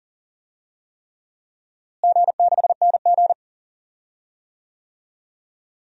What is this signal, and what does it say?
Morse code, heard as a keyed tone, about 40 wpm, 700 Hz. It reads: G6NG
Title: DXCC-100 Call Signs - 4 Characters - Repeated 3x - 40wpm